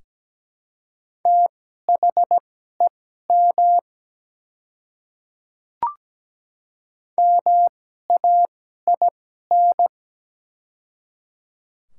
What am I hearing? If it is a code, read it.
Morse code, heard as a keyed tone, about 17 wpm, spaced as for 12 wpm, 700 Hz. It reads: THEM MAIN
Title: 2x Character Spacing - Top 300 Words - 17wpm